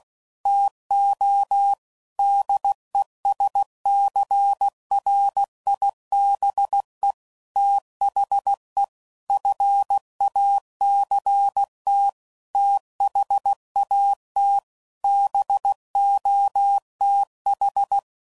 Morse code, heard as a keyed tone, about 16 wpm, 800 Hz. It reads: TO DESCRIBE THE FACT THAT BOTH
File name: CW Code Groups @15WPM / Technical Subjects / Reactance